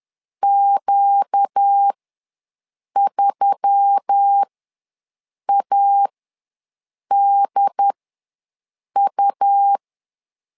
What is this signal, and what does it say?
Morse code, heard as a keyed tone, about 10 wpm, spaced as for 6 wpm, 800 Hz. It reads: Q3ADU
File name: LEZIONI / 4 Q1S3 ws 10wpm 2437